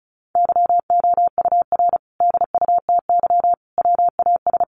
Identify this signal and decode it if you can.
Morse code, heard as a keyed tone, about 34 wpm, 700 Hz. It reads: YOUR DUTY WAS
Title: Sentences from Top 1500 Words 35wpm